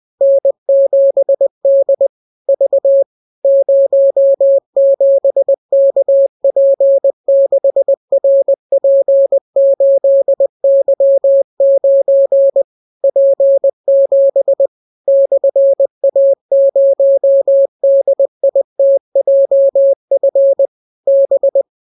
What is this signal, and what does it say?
Morse code, heard as a keyed tone, about 20 wpm, 550 Hz. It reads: N7D V 07KP6RP8Y9 P7 /A0DITJF B